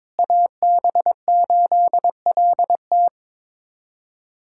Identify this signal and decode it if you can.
Morse code, heard as a keyed tone, about 21 wpm, 700 Hz. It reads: AB8LT
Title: USA Callsign DE Callsign - 5 Characters - 22wpm